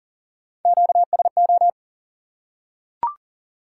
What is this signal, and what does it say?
Morse code, heard as a keyed tone, about 40 wpm, 700 Hz. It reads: QSO
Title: Top 180 QSO Elements 40wpm